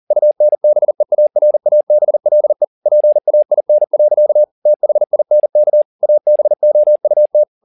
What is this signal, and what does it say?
Morse code, heard as a keyed tone, about 40 wpm, 600 Hz. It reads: UNBEARABLE PAIN. THINK ABOUT
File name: the old man and the sea 2 40wpm